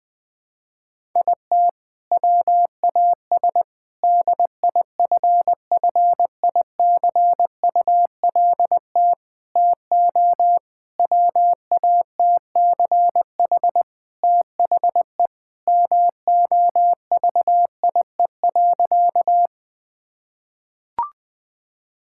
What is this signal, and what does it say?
Morse code, heard as a keyed tone, about 21 wpm, 700 Hz. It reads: IT WAS DIFFICULT TO WATCH THE MOVIE.
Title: Sentences from Top 1000 Words - Review Part 5 - 20wpm